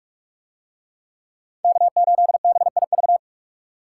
Morse code, heard as a keyed tone, about 46 wpm, 700 Hz. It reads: K8BIV